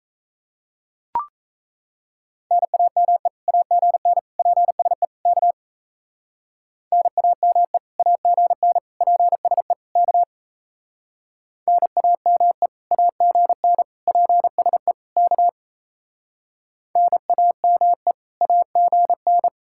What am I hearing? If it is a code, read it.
Morse code, heard as a keyed tone, about 35 wpm, 700 Hz. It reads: NAME AGN PSE K NAME AGN PSE K NAME AGN PSE K NAME AGN